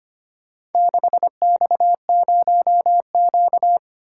Morse code, heard as a keyed tone, about 25 wpm, 700 Hz. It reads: 6X0Q
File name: practice / 25wpm / Four Letter-Number Combinations - 25wpm